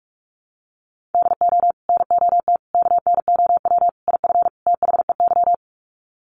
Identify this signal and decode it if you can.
Morse code, heard as a keyed tone, about 44 wpm, 700 Hz. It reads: DO NOT KNOW IF THEY
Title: Sets of 5 Words Encore 45wpm